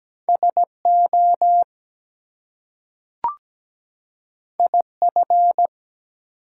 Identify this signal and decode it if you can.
Morse code, heard as a keyed tone, about 17 wpm, 700 Hz. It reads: SO IF
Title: Two Letter Words - 17wpm